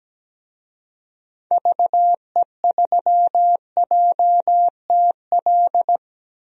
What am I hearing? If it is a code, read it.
Morse code, heard as a keyed tone, about 17 wpm, 700 Hz. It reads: VE3JTL
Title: DXCC-100 Call Signs - 6 Characters - 17wpm